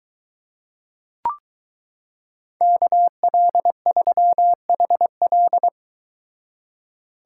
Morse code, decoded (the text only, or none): KL3HL